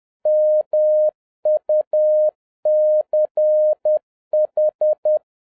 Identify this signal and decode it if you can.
Morse code, heard as a keyed tone, about 10 wpm, 610 Hz. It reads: MUCH
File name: 2011-07-mp3 / n56 BACK FROM A BIG ACCIDENT 10wpm